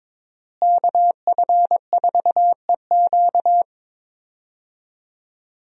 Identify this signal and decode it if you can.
Morse code, heard as a keyed tone, about 22 wpm, 700 Hz. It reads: KF4EQ
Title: USA Call Signs - 5 Characters - Repeated 2x - 22wpm